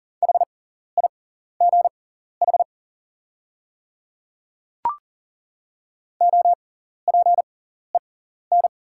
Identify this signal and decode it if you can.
Morse code, heard as a keyed tone, about 40 wpm, spaced as for 14 wpm, 690 Hz. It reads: HIGH OPEN
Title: 6x Character Spacing - Top 300 Words - 40wpm